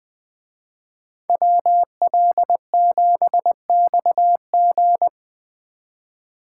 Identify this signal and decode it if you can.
Morse code, heard as a keyed tone, about 20 wpm, 700 Hz. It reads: WL7XG